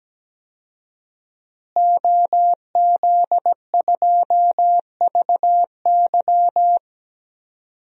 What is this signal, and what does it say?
Morse code, heard as a keyed tone, about 17 wpm, 700 Hz. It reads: OZ2VY
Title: DXCC-100 Call Signs - 5 Characters - Repeated 3x - 17wpm